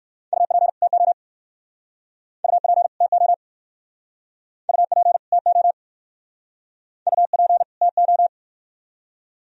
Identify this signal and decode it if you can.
Morse code, heard as a keyed tone, about 55 wpm, 690 Hz. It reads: UP TO UP TO UP TO UP TO
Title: Speed-Racing - Sets of 2 Words 45wpm